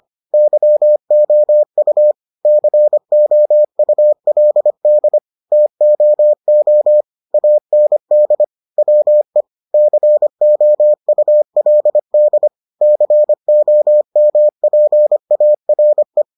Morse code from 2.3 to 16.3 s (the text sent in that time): COULD TOO AND WE COULD COMPARE